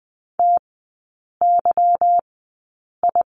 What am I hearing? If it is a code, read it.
Morse code, heard as a keyed tone, about 20 wpm, spaced as for 8 wpm, 700 Hz. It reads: TYI
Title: 145 Lesson 24e - Letter y - Three Characters 20wpm